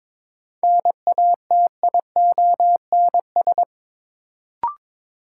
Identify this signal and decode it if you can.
Morse code, heard as a keyed tone, about 22 wpm, 700 Hz. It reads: NATIONS
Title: Sets of 3 Words - New Word Drill - 22wpm